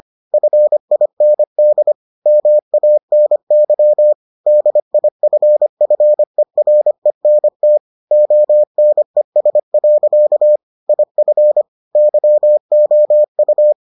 Morse code, heard as a keyed tone, about 25 wpm, 600 Hz. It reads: FIND MANY DIFFERENT ONES. IF YOU